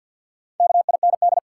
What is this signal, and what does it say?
Morse code, heard as a keyed tone, about 53 wpm, 700 Hz. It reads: KIND
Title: Sets of 2 Words - New Word Drill - 50wpm